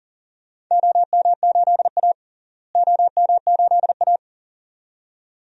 Morse code, heard as a keyed tone, about 37 wpm, 700 Hz. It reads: OM8A OM8A
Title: DXCC-100 Call Signs - 4 Characters - Repeated 2x - 40wpm